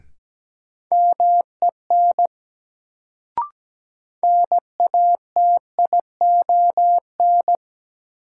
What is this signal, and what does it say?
Morse code, heard as a keyed tone, about 17 wpm, 700 Hz. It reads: MEN NATION
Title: Sets of 3 Words - New Word Drill - 17wpm